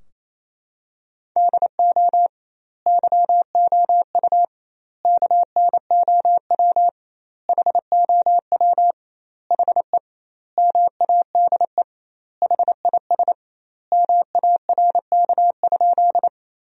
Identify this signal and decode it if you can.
Morse code, heard as a keyed tone, about 28 wpm, 700 Hz. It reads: DO YOU KNOW HOW HE MADE HIS MARK?
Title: 2x Word-Spacing - Sentences from Top 300 Words - 28wpm